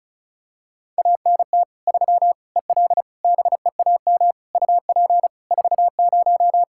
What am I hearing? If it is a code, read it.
Morse code, heard as a keyed tone, about 36 wpm, 700 Hz. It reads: ANT 3 EL BEAM UP 40
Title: K1USN SST QSO Practice 35wpm